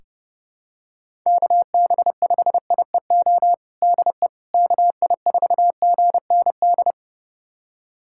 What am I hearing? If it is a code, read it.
Morse code, heard as a keyed tone, about 32 wpm, 700 Hz. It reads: KB5IEO DE KI4GND
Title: USA Callsign DE Callsign 30wpm